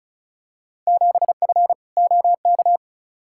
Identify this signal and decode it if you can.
Morse code, heard as a keyed tone, about 36 wpm, 700 Hz. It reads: 7F OK
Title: Field Day Exchanges Encore - 35wpm